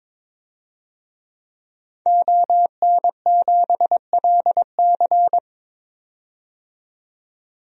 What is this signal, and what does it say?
Morse code, heard as a keyed tone, about 21 wpm, 700 Hz. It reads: ON7LC
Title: DXCC-100 Call Signs - 5 Characters - 22wpm